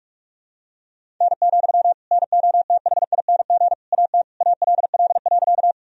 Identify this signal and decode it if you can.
Morse code, heard as a keyed tone, about 41 wpm, 700 Hz. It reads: N, NOTHING AT ALL.
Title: Sentences from Top 500 Words and Abbreviations 45wpm